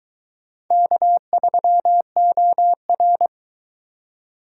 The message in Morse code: K3OR

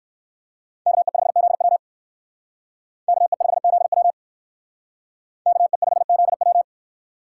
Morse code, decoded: KE5ZW KE5ZW KE5ZW

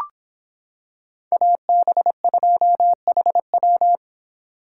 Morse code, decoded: AB2HW

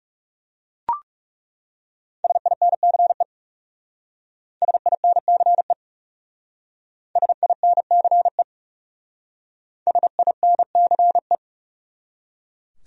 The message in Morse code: SINCE SINCE SINCE SINCE

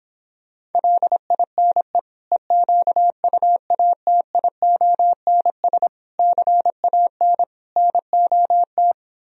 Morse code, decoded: LINE EQUATIONS CAN NOT

